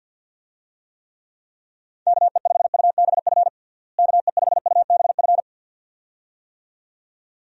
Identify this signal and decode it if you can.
Morse code, heard as a keyed tone, about 51 wpm, 700 Hz. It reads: KE5UBF KE5UBF
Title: USA Call Signs - 6 Characters - Repeated 2x - 50wpm